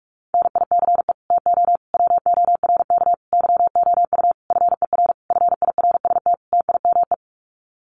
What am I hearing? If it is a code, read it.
Morse code, heard as a keyed tone, about 44 wpm, 700 Hz. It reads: NICE TO WORK YOU FER FIRST TIME
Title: Common QSO Phrases 45wpm